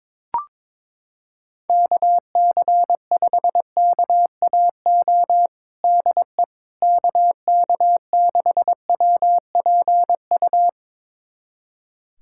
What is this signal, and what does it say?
Morse code, heard as a keyed tone, about 22 wpm, 700 Hz. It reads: KC5KAO DE KK6WPU
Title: USA Callsign DE Callsign 22wpm